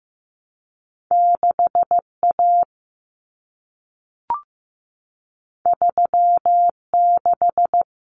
Morse code, heard as a keyed tone, about 15 wpm, 700 Hz. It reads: 6A 36
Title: Common Cut Numbers - 2-Digit 15wpm